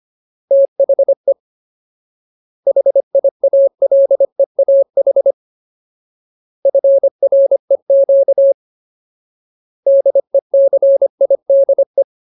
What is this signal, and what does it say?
Morse code, heard as a keyed tone, about 25 wpm, 550 Hz. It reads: THE HIALEAH FREQ DECIDE
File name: Ultimate Word List / 25WPM Ultimate